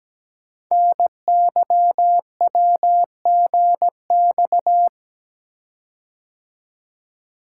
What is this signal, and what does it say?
Morse code, heard as a keyed tone, about 17 wpm, 700 Hz. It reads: NYWGX
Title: Code Groups - 5 Letter:Numbers - 17wpm